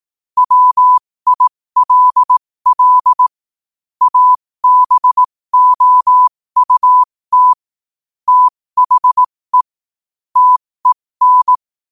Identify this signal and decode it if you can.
Morse code, heard as a keyed tone, about 18 wpm, spaced as for 15 wpm, 1000 Hz. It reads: WILL ABOUT THE TEN
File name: qotdfiles / 15wpm